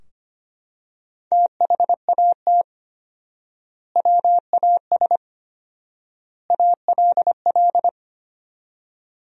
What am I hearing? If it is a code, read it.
Morse code, heard as a keyed tone, about 25 wpm, 700 Hz. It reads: THAT WAS ALL
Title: 4x Word-Spacing - Sentences from Top 100 Words - 25wpm